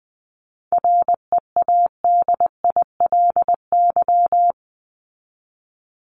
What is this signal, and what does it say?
Morse code, heard as a keyed tone, about 20 wpm, 700 Hz. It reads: READILY
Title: LY Suffix - Top 100 Words with Base Words 20wpm